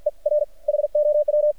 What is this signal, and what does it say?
Morse code, heard as a keyed tone, about 50 wpm, 600 Hz. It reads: E R HOW
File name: archived / CRSnet